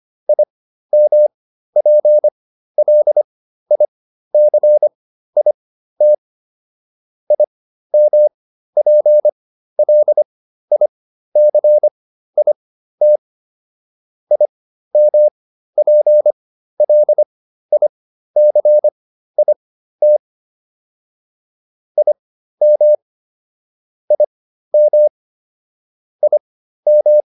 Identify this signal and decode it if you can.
Morse code, heard as a keyed tone, about 25 wpm, spaced as for 13 wpm, 600 Hz. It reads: IMPLICIT IMPLICIT IMPLICIT IM IM IM